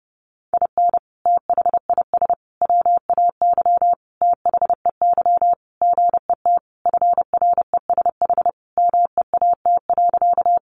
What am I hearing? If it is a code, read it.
Morse code, heard as a keyed tone, about 30 wpm, 700 Hz. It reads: IN THIS WAY THEY GET FRESH MEAT.
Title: Sentences from Top 1200 Words 30wpm